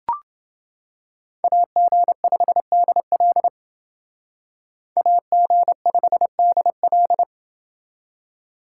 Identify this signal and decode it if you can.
Morse code, heard as a keyed tone, about 29 wpm, 700 Hz. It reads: AG5DL AG5DL